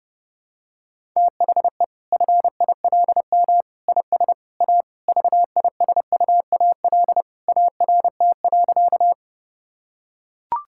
Morse code, heard as a keyed tone, about 30 wpm, 700 Hz. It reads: THE FILM IS A VISUAL ART.
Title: Sentences from Top 1000 Words - Review Part 6 - 30wpm